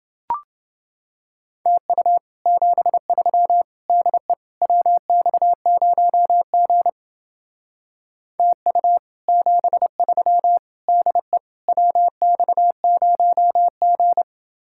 Morse code, heard as a keyed tone, about 28 wpm, 700 Hz. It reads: TU 73 DE WX0G TU 73 DE WX0G